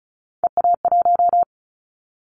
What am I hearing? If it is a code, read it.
Morse code, heard as a keyed tone, about 34 wpm, 700 Hz. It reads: EA1
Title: Callsign Prefixes for Top 25 DXCC - 35wpm